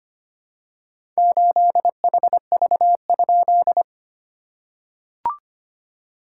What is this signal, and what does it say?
Morse code, heard as a keyed tone, about 25 wpm, 690 Hz. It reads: 8HV?
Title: SOTA QSO Practice 25wpm